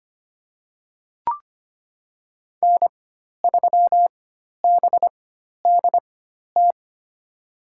Morse code, heard as a keyed tone, about 25 wpm, spaced as for 12 wpm, 700 Hz. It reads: N3BDT
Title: USA Call Signs - 5 Characters - 4x Character Spacing - 25wpm